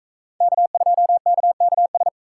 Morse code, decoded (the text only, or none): K2KKS